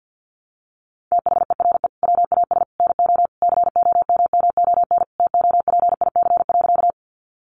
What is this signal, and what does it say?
Morse code, heard as a keyed tone, about 50 wpm, 700 Hz. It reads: THERE WAS NO COMMON TOPIC.
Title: Sentences from Top 1200 Words 50wpm